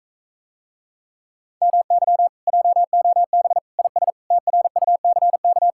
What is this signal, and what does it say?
Morse code, heard as a keyed tone, about 45 wpm, 700 Hz. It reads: MY JOB IS TRUCK